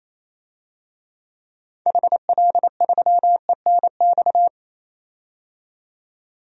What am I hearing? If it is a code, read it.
Morse code, heard as a keyed tone, about 28 wpm, 700 Hz. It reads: HL3ENX
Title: DXCC-100 Call Signs - 6 Characters - 28wpm